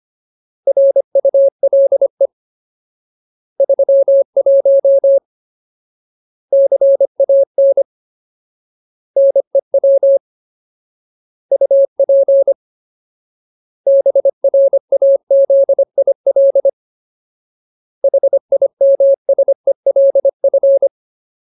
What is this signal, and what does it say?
Morse code, heard as a keyed tone, about 25 wpm, 550 Hz. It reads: RULE 31 CAN NEW UP BRAZIL HIMSELF
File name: Ultimate Word List / 25WPM Ultimate